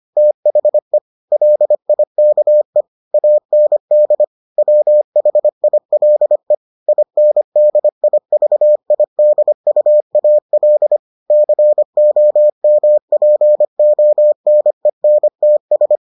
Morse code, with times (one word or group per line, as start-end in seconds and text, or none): 0.2-1.0 THE
1.3-2.8 LIKE
3.1-4.2 AND
4.6-6.5 WHILE
6.9-11.0 INDIVIDUAL
11.3-16.0 COMPONENTS